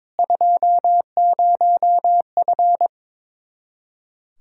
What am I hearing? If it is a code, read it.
Morse code, heard as a keyed tone, about 22 wpm, 700 Hz. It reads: 20F
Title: Speed-Racing Common QSO Phrases Encore - 22wpm